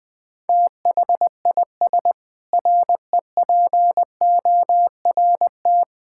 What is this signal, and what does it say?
Morse code, heard as a keyed tone, about 21 wpm, 700 Hz. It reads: THIS REPORT
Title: Sentences with 7-Word Phrases - 20wpm